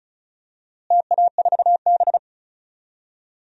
Morse code, heard as a keyed tone, about 34 wpm, 700 Hz. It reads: TA4B
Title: DXCC-100 Call Signs - 4 Characters - Repeated 2x - 35wpm